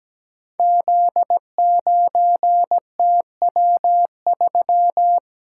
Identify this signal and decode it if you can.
Morse code, heard as a keyed tone, about 17 wpm, 700 Hz. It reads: Z9TW3
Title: Code Groups - 5 Letter:Numbers - 17wpm